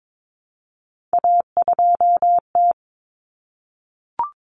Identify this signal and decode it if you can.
Morse code, heard as a keyed tone, about 22 wpm, 700 Hz. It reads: A2T
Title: Common Cut Numbers - 3-Digit 22wpm